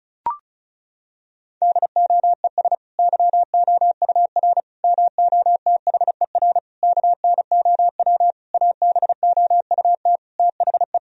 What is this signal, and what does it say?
Morse code, heard as a keyed tone, about 34 wpm, 700 Hz. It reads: DOES YOUR MOTHER KNOW ABOUT THE
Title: Sentences from Top 1000 Words - Review Part 4 - 35wpm